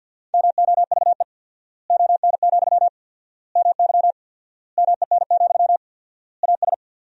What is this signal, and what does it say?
Morse code, heard as a keyed tone, about 53 wpm, 700 Hz. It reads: MOVE ON, MY MEN, AS
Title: 4x Word-Spacing - Sentences from Top 500 Words - 50wpm